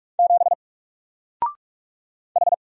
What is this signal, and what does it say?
Morse code, heard as a keyed tone, about 44 wpm, 700 Hz. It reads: 7 H
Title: Single Letter-Number Alt 45wpm